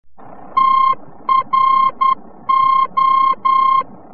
Morse code, heard as a keyed tone, about 10 wpm, 1060 Hz. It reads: TRO
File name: france / mx / TRO-VOR-F